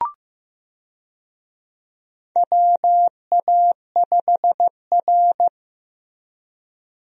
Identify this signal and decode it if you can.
Morse code, heard as a keyed tone, about 15 wpm, 700 Hz. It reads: WA5R
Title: USA Call Signs - 4 Characters - Repeated 2x - 15wpm